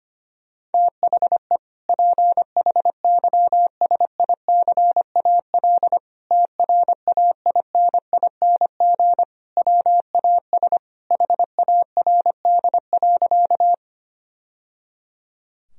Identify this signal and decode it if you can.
Morse code, heard as a keyed tone, about 25 wpm, 700 Hz. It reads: THE PHYSICAL TRAINING WAS HARD.
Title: Sentences from Top 1000 Words - Review Part 4 - 25wpm